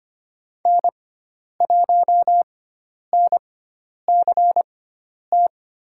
Morse code, heard as a keyed tone, about 25 wpm, spaced as for 10 wpm, 700 Hz. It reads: N1NCT
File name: Lesson 16g - Letter c - Call Signs 25wpm